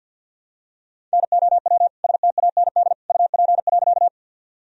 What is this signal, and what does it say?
Morse code, heard as a keyed tone, about 52 wpm, 700 Hz. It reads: NOW STAND UP.
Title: Sentences from Top 1000 Words - Review Part 1 - 50wpm